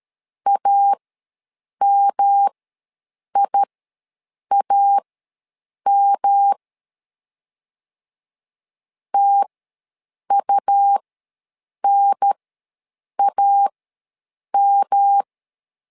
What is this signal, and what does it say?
Morse code, heard as a keyed tone, about 13 wpm, spaced as for 7 wpm, 800 Hz. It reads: AMIAM TUNAM
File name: LEZIONI / 3 MWRO / 3 M ws 12wpm 1745